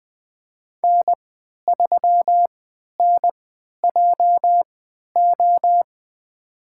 Morse code, heard as a keyed tone, about 20 wpm, spaced as for 11 wpm, 700 Hz. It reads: N3NJO
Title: USA Call Signs - 5 Characters - 3x Character Spacing - 20wpm